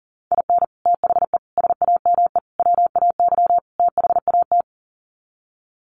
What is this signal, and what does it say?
Morse code, heard as a keyed tone, about 37 wpm, 700 Hz. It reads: IN THE SAME WAY THAT
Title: Sets of 5 Words 40wpm